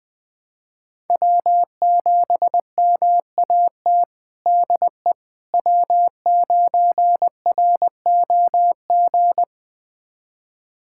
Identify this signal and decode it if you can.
Morse code, heard as a keyed tone, about 20 wpm, 700 Hz. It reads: W7MAT DE W9ROG
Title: USA Callsign DE Callsign - 5 Characters - 20wpm